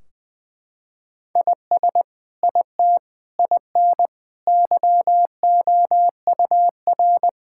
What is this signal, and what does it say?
Morse code, heard as a keyed tone, about 20 wpm, 700 Hz. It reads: IS IT IN YOUR
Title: Sentences from Top 1600 Words 20wpm